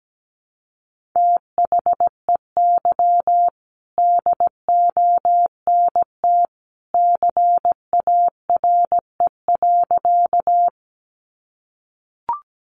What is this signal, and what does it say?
Morse code, heard as a keyed tone, about 17 wpm, 700 Hz. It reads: THEY DONT CARE.
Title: Sentences from Top 300 Words 17wpm